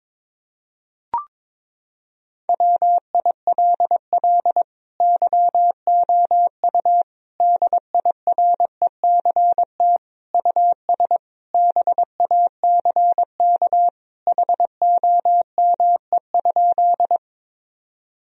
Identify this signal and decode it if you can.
Morse code, heard as a keyed tone, about 21 wpm, 700 Hz. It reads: WILL YOU DIRECT US BACK HOME?
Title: Sentences from Top 1000 Words - Review Part 2 - 22wpm